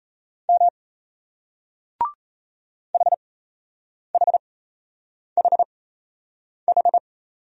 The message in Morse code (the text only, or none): M HHHH